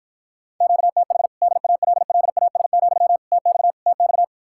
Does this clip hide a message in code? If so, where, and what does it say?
Yes: Morse code from 0.6 to 4.2 s, heard as a keyed tone, about 52 wpm, 700 Hz. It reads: QTH DALLAS, TX TX